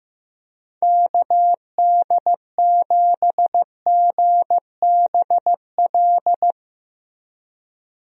Morse code, heard as a keyed tone, about 15 wpm, 700 Hz. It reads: KD7GBL
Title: USA Call Signs - 6 Characters - 15wpm